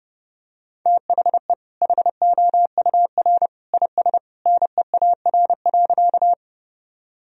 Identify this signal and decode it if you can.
Morse code, heard as a keyed tone, about 30 wpm, 700 Hz. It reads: THE HOUR IS NEAR.
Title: Sentences from Top 1000 Words - Review Part 1 - 30wpm